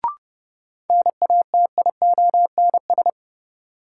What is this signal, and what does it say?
Morse code, heard as a keyed tone, about 30 wpm, 700 Hz. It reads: NATIONS
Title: Sets of 3 Words - New Word Drill - 30wpm